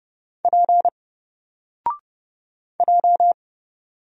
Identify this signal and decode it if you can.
Morse code, heard as a keyed tone, about 30 wpm, 700 Hz. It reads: P J
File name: Single-Letter-Number-30wpm